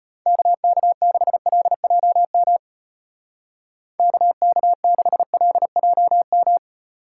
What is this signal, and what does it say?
Morse code, heard as a keyed tone, about 32 wpm, 700 Hz. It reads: KK6LJM KK6LJM